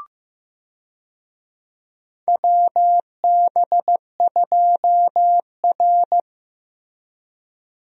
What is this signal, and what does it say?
Morse code, heard as a keyed tone, about 15 wpm, 700 Hz. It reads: WB2R
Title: USA Call Signs - 4 Characters - Repeated 3x - 15wpm